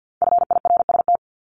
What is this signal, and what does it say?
Morse code, heard as a keyed tone, about 50 wpm, 700 Hz. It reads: FIRST
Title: Multi-letter Phonemes 50wpm